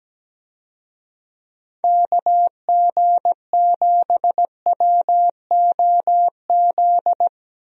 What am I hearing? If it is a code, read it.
Morse code, heard as a keyed tone, about 17 wpm, 700 Hz. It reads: KG7WOZ
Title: USA Call Signs - 6 Characters - 17wpm